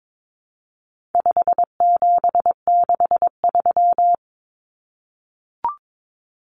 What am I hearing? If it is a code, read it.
Morse code, heard as a keyed tone, about 22 wpm, 700 Hz. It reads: 5763
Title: Common Cut Numbers - 4-Digit 22wpm